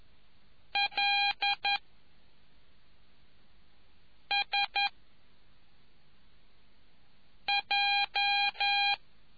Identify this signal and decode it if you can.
Morse code, heard as a keyed tone, about 11 wpm, 2400 Hz. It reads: L S J